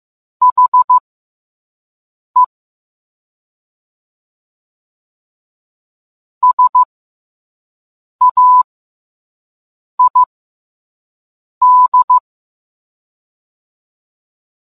Morse code, heard as a keyed tone, about 15 wpm, spaced as for 5 wpm, 1000 Hz. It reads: HE SAID